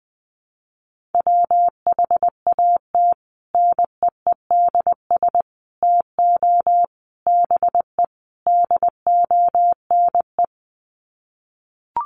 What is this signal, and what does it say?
Morse code, heard as a keyed tone, about 20 wpm, 700 Hz. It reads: WHAT NEEDS TO BE DONE
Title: Sets of 5 Words Encore 20wpm